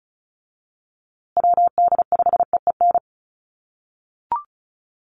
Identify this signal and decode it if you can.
Morse code, heard as a keyed tone, about 34 wpm, 700 Hz. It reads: WD5EEN
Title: USA Call Signs Repeated 3x 35wpm